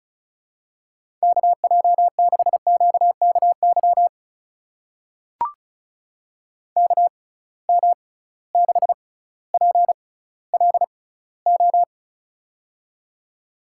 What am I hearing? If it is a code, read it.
Morse code, heard as a keyed tone, about 36 wpm, spaced as for 18 wpm, 700 Hz. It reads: KJ6QKY KM6PLO
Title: USA Call Signs - 6 Characters - 6x Character Spacing - 35wpm